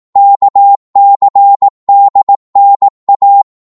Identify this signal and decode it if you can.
Morse code, heard as a keyed tone, about 18 wpm, 800 Hz. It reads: KCDNA
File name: Morse / ltr-18wpm